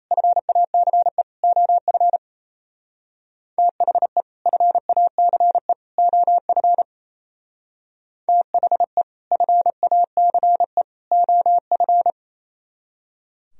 Morse code, heard as a keyed tone, about 37 wpm, 700 Hz. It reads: FACE OF THE FACE OF THE FACE OF